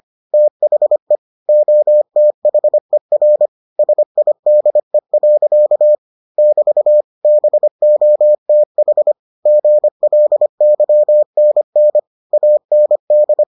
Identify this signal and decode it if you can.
Morse code, heard as a keyed tone, about 25 wpm, 600 Hz. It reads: THE OTHER SIDE. <BT> BOTH GLYNN AND